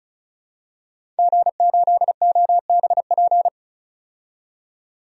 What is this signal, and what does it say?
Morse code, heard as a keyed tone, about 36 wpm, 700 Hz. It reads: G8OBP
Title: DXCC-100 Call Signs - 5 Characters - 35wpm